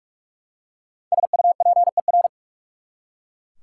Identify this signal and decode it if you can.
Morse code, heard as a keyed tone, about 44 wpm, 700 Hz. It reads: SUPER
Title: Prefixes and Suffixes 45wpm